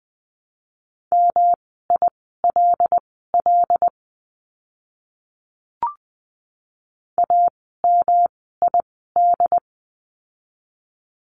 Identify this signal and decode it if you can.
Morse code, heard as a keyed tone, about 20 wpm, spaced as for 14 wpm, 700 Hz. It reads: MILL AMID
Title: 108 Lesson 18f - Letter m - Words 20wpm